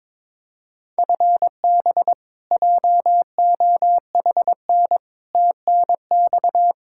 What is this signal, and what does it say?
Morse code, heard as a keyed tone, about 22 wpm, 700 Hz. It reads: FB JOHN TNX
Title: Practice QSO - Exchange 3 - More Info - 22wpm